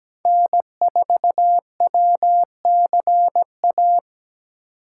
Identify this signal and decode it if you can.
Morse code, heard as a keyed tone, about 17 wpm, 700 Hz. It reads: N4WCA
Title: USA Call Signs - 5 Characters - 2x Character Spacing - 17wpm